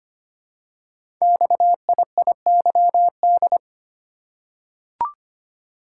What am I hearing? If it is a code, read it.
Morse code, heard as a keyed tone, about 25 wpm, 700 Hz. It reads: XIIYD
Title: Code Groups - 5 Letter:Numbers - 25wpm